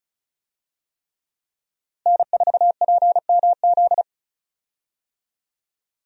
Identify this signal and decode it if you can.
Morse code, heard as a keyed tone, about 36 wpm, 700 Hz. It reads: N4PMZ
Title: USA Call Signs - 5 Characters - Repeated 3x - 35wpm